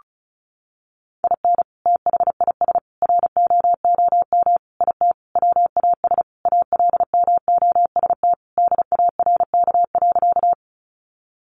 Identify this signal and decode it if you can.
Morse code, heard as a keyed tone, about 34 wpm, 700 Hz. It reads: IN THIS ROOM IT WAS ALMOST DARK.
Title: Sentences from Top 700 Words 35wpm